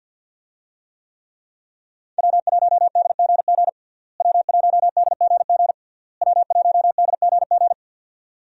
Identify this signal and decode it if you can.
Morse code, heard as a keyed tone, about 52 wpm, 690 Hz. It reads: W1DGG W1DGG W1DGG
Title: USA Call Signs - 5 Characters - Repeated 3x - 50wpm